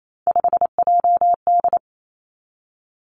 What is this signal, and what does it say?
Morse code, heard as a keyed tone, about 28 wpm, 700 Hz. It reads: 5JD
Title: Three Challenging Letter-Number Combinations 28wpm